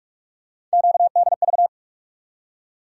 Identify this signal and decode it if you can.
Morse code, heard as a keyed tone, about 47 wpm, 700 Hz. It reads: QDV
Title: USA Call Sign Postfixes - 3 Letters 45wpm